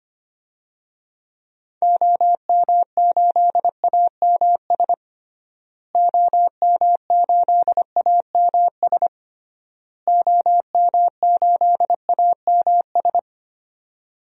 Morse code, heard as a keyed tone, about 25 wpm, 700 Hz. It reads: OM8AMS OM8AMS OM8AMS
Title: DXCC-100 Call Signs - 6 Characters - Repeated 3x - 25wpm